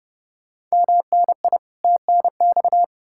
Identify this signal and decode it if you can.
Morse code, heard as a keyed tone, about 40 wpm, 700 Hz. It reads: MNI TNX